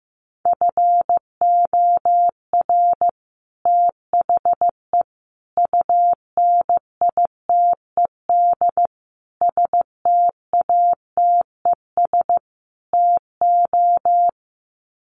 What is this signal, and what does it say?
Morse code, heard as a keyed tone, about 15 wpm, 700 Hz. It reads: FOR THE UNITED STATES TO
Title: Sets of 5 Words Encore 15wpm